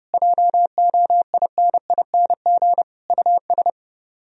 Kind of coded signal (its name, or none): Morse code